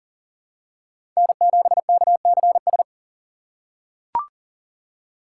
Morse code, heard as a keyed tone, about 38 wpm, 700 Hz. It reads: N7KCS
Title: USA Call Signs - 3x Character Spacing - 40wpm